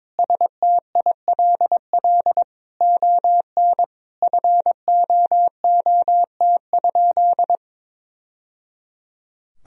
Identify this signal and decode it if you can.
Morse code, heard as a keyed tone, about 22 wpm, 700 Hz. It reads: STILL ON FOOT?
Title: Sentences from Top 1000 Words - Review Part 3 - 22wpm